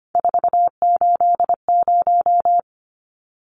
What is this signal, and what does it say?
Morse code, heard as a keyed tone, about 25 wpm, 700 Hz. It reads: 480
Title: Numbers - 3-Digit 25wpm